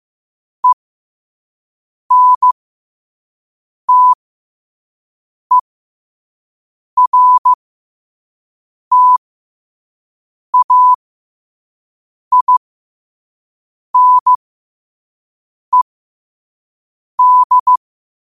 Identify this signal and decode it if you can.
Morse code, heard as a keyed tone, about 15 wpm, spaced as for 5 wpm, 1000 Hz. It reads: ENTERTAINED
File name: qotdfiles / 5wpm